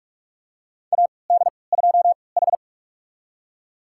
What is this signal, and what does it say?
Morse code, heard as a keyed tone, about 46 wpm, spaced as for 26 wpm, 700 Hz. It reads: AD2H
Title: USA Call Signs - 4 Characters - 3x Character Spacing - 45wpm